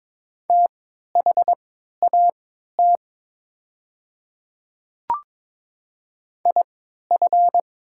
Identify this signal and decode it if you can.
Morse code, heard as a keyed tone, about 22 wpm, spaced as for 12 wpm, 700 Hz. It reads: THAT IF
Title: 3x Character Spacing - Top 100 Words - 22wpm